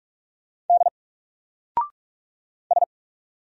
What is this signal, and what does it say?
Morse code, heard as a keyed tone, about 44 wpm, 700 Hz. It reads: D S
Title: Single Character (Letter:Number:Punctuation:Prosign) 45wpm